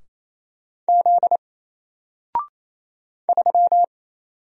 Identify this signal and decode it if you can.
Morse code, heard as a keyed tone, about 29 wpm, 700 Hz. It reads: Z 3
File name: Single-Letter-Number-28wpm